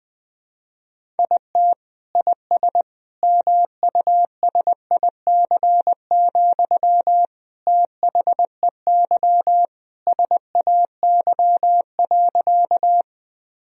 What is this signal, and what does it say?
Morse code, heard as a keyed tone, about 21 wpm, 700 Hz. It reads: IT IS MUSIC, THEY SAY.
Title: Sentences from Top 1000 Words - Review Part 1 - 20wpm